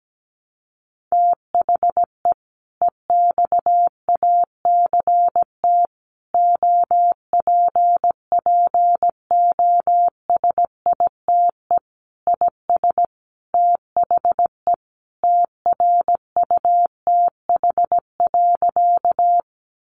Morse code, heard as a keyed tone, about 17 wpm, 700 Hz. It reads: THE EXACT OPPOSITE IS THE TRUTH.
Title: Sentences from Top 1300 Words 17wpm